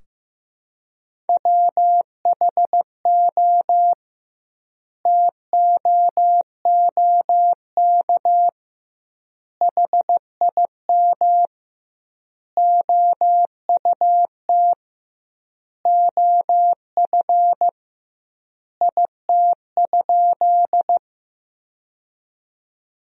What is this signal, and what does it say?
Morse code, heard as a keyed tone, about 15 wpm, 700 Hz. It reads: WHO TOOK HIM OUT OF IT?
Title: 2x Word-Spacing - Sentences from Top 300 Words - 15wpm